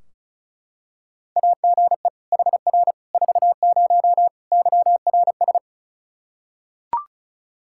Morse code, heard as a keyed tone, about 36 wpm, 700 Hz. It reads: AGE HR 40 YRS
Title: Practice QSO - Exchange 3 - More Info - 35wpm